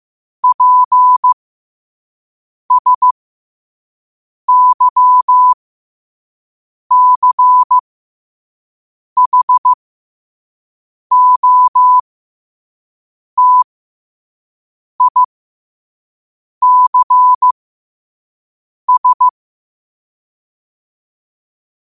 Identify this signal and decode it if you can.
Morse code, heard as a keyed tone, about 15 wpm, spaced as for 5 wpm, 1000 Hz. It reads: PSYCHOTICS
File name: qotdfiles / 5wpm